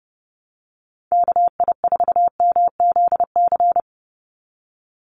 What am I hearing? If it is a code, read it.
Morse code, heard as a keyed tone, about 30 wpm, 700 Hz. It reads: KI4MZC
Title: Call Signs Repeated 2x - 30wpm